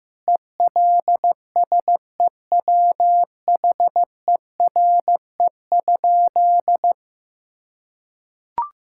Morse code, heard as a keyed tone, about 15 wpm, 700 Hz. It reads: ELSEWHERE?
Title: Sentences from Top 1600 Words 15wpm